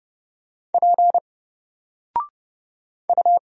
Morse code, heard as a keyed tone, about 30 wpm, 700 Hz. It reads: P U
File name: Single-Letter-Number-30wpm